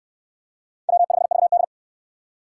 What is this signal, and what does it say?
Morse code, heard as a keyed tone, about 67 wpm, 700 Hz. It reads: W5VD